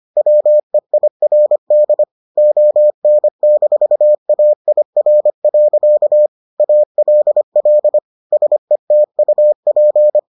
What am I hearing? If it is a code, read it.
Morse code, heard as a keyed tone, about 25 wpm, 600 Hz. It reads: WEIRD ON-AIR. ALL SETUP